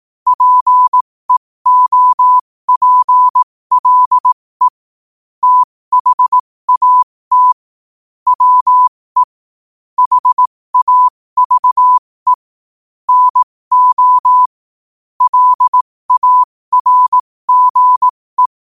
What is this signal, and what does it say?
Morse code, heard as a keyed tone, about 18 wpm, spaced as for 15 wpm, 1000 Hz. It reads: PEOPLE THAT WE HAVE NO LARGE
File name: qotdfiles / 15wpm